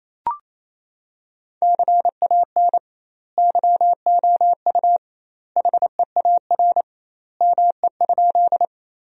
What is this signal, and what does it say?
Morse code, heard as a keyed tone, about 28 wpm, 700 Hz. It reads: CAN YOU HEAR ME?
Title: 2x Word-Spacing - Sentences from Top 300 Words - 28wpm